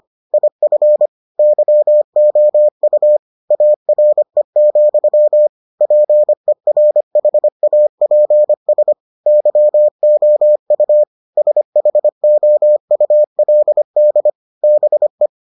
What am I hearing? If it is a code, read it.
Morse code, heard as a keyed tone, about 25 wpm, 600 Hz. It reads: IF YOU ARE, PERHAPS YOU SHOULD BE